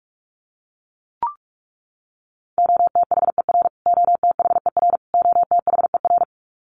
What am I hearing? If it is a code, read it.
Morse code, heard as a keyed tone, about 44 wpm, 700 Hz. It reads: OTHER OTHER OTHER
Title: Top 200 Words Repeated 45wpm